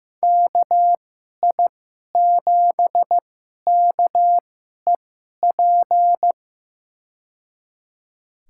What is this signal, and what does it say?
Morse code, heard as a keyed tone, about 15 wpm, spaced as for 11 wpm, 700 Hz. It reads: KI7KEP
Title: USA Call Signs - 6 Characters - 2x Character Spacing - 15wpm